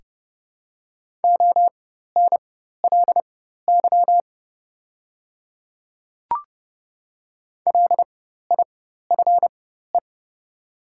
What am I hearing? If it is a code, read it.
Morse code, heard as a keyed tone, about 30 wpm, spaced as for 14 wpm, 700 Hz. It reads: ONLY LIFE
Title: 4x Character Spacing - Top 500 Words - 30wpm